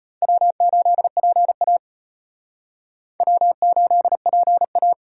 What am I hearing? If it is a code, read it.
Morse code, heard as a keyed tone, about 31 wpm, 700 Hz. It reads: W8PA W8PA